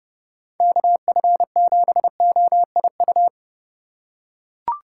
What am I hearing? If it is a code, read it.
Morse code, heard as a keyed tone, about 30 wpm, 700 Hz. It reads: KF7OIU
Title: USA Call Signs - 2x Character Spacing - 30wpm